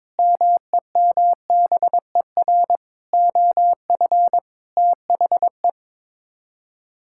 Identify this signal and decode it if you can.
Morse code, heard as a keyed tone, about 22 wpm, 700 Hz. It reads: MEMBER OF THE
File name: practice / 22wpm / Sets of 6 Words - 22wpm